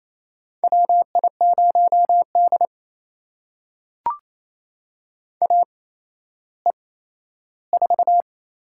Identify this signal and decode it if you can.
Morse code, heard as a keyed tone, about 27 wpm, spaced as for 14 wpm, 700 Hz. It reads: WI0D AE4
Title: USA Call Signs - 4 Characters - 8x Character Spacing - 28wpm